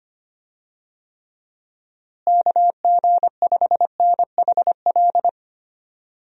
Morse code, heard as a keyed tone, about 25 wpm, 700 Hz. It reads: KG5NHL
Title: USA Call Signs - 6 Characters - Repeated 2x - 25wpm